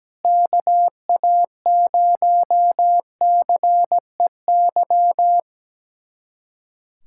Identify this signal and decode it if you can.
Morse code, heard as a keyed tone, about 17 wpm, 700 Hz. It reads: KA0CEY
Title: USA Callsign DE Callsign 17wpm